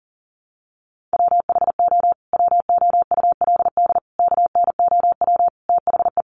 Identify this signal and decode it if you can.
Morse code, heard as a keyed tone, about 40 wpm, 700 Hz. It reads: WHO WOULD KNOW THE
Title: Sentences from Top 800 Words 40wpm